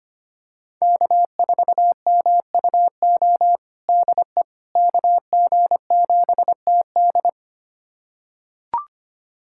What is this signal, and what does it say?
Morse code, heard as a keyed tone, about 25 wpm, 700 Hz. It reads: K4MUO DE KG7TD
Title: USA Callsign DE Callsign - 5 Characters - 25wpm